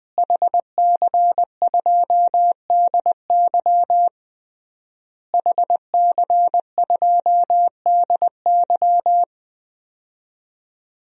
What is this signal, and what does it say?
Morse code, heard as a keyed tone, about 20 wpm, 700 Hz. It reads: HC2DY HC2DY
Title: DXCC-100 Call Signs - 5 Characters - Repeated 3x - 20wpm